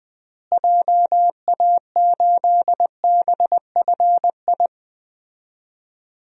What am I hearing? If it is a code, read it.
Morse code, heard as a keyed tone, about 20 wpm, 700 Hz. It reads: JA8BFI
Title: DXCC-100 Call Signs - 6 Characters - Repeated 3x - 20wpm